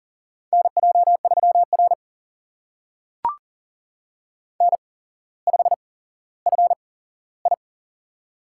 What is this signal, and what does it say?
Morse code, heard as a keyed tone, about 40 wpm, spaced as for 17 wpm, 700 Hz. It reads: NJ3R N5FI
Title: USA Call Signs - 4 Characters - 8x Character Spacing - 40wpm